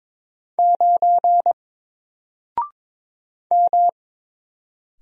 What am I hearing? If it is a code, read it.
Morse code, heard as a keyed tone, about 22 wpm, 700 Hz. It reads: 9 M
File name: Single-Letter-Number-22wpm